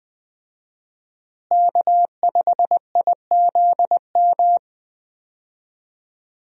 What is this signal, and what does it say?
Morse code, heard as a keyed tone, about 20 wpm, 700 Hz. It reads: K5IZM
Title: USA Call Signs - 5 Characters - Repeated 3x - 20wpm